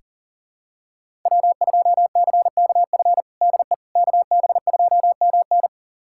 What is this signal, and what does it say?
Morse code, heard as a keyed tone, about 40 wpm, 700 Hz. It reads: W2CKF DE KB2MN
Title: USA Callsign DE Callsign - 5 Characters - 40wpm